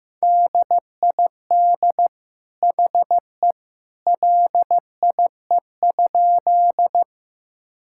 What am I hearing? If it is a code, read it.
Morse code, heard as a keyed tone, about 16 wpm, 700 Hz. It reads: DID HE LIE?
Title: Sentences from Top 1600 Words 15wpm